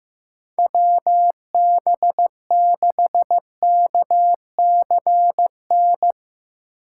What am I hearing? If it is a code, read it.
Morse code, heard as a keyed tone, about 15 wpm, 700 Hz. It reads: WB6KCN
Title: USA Call Signs - 6 Characters - 2x Character Spacing - 15wpm